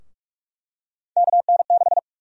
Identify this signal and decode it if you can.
Morse code, heard as a keyed tone, about 46 wpm, 700 Hz. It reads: KN6
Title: USA Call Sign Prefixes - Repeated 2x - 45wpm